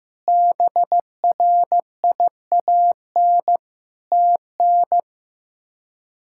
Morse code, heard as a keyed tone, about 15 wpm, 700 Hz. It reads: BRIAN TN
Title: K1USN SST QSO Practice - 15wpm